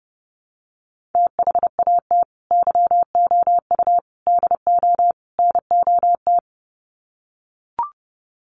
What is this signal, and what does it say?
Morse code, heard as a keyed tone, about 30 wpm, 700 Hz. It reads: THAT YOU DO NOT
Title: Sets of 4 Words 30wpm